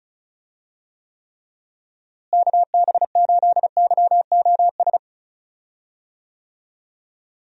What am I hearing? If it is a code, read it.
Morse code, heard as a keyed tone, about 34 wpm, 700 Hz. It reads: KB8YOS
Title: USA Call Signs - 6 Characters - Repeated 2x - 35wpm